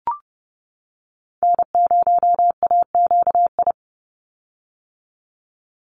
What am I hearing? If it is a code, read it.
Morse code, heard as a keyed tone, about 30 wpm, 700 Hz. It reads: N0AQI
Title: USA Call Signs 30wpm